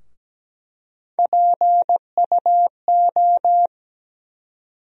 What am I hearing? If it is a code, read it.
Morse code, heard as a keyed tone, about 17 wpm, 700 Hz. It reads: PUO
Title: USA Call Sign Postfixes - 3 Letters 17wpm